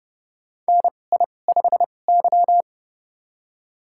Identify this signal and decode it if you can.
Morse code, heard as a keyed tone, about 29 wpm, spaced as for 21 wpm, 700 Hz. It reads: NI5Y
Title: USA Call Signs - 2x Character Spacing - 30wpm